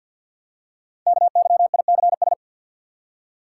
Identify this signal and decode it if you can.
Morse code, heard as a keyed tone, about 49 wpm, 700 Hz. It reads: KYICS
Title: Code Groups - 5 Letters - 50wpm